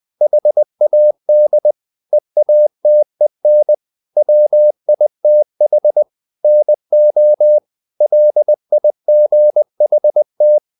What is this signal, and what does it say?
Morse code, heard as a keyed tone, about 20 wpm, 600 Hz. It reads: HAD EATEN WITH NO LIGHT
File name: the old man and the sea / the old man and the sea part 1 20wpm